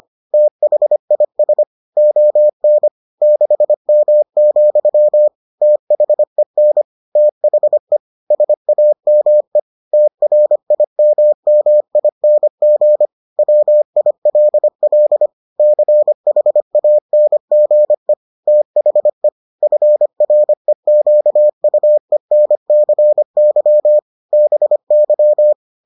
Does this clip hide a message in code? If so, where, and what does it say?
Yes: Morse code from 0.3 to 25.5 s, heard as a keyed tone, about 25 wpm, 600 Hz. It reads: THIS ON 6M, THEN THE SAME TRIMMING WILL CHANGE THE FREQUENCY BY